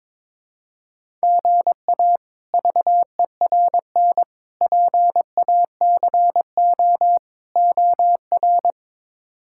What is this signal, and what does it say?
Morse code, heard as a keyed tone, about 21 wpm, 700 Hz. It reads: GA VERN PACO OR
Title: K1USN SST QSO Practice - 22wpm